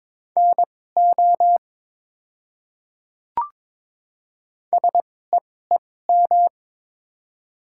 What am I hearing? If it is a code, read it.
Morse code, heard as a keyed tone, about 23 wpm, spaced as for 16 wpm, 700 Hz. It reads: NO SEEM
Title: 2x Character Spacing - Top 300 Words - 22wpm